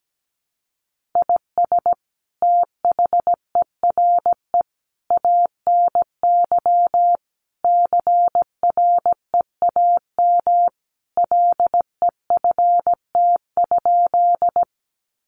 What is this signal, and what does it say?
Morse code, heard as a keyed tone, about 17 wpm, 700 Hz. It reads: IS THERE ANY CREAM LEFT?
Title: Sentences from Top 1500 Words 17wpm